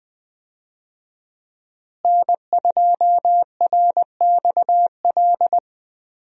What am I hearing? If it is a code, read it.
Morse code, heard as a keyed tone, about 20 wpm, 700 Hz. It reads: N2RXL
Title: USA Call Signs - 5 Characters - Repeated 2x - 20wpm